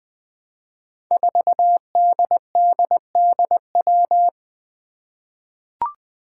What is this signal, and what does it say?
Morse code, heard as a keyed tone, about 21 wpm, 700 Hz. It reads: 4DDDW
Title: Five Challenging Letter-Number Combinations 20wpm